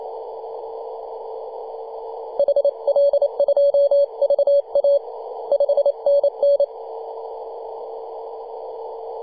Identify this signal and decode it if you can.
Morse code, heard as a keyed tone, about 28 wpm, 570 Hz. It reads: HL2VA 5NN
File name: 2015/May/15 2222z PW0F 21.026MHz CW